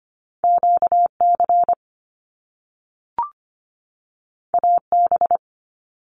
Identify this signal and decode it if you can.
Morse code, heard as a keyed tone, about 25 wpm, 700 Hz. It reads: QC AB
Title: Canadian Provinces and Territories 25wpm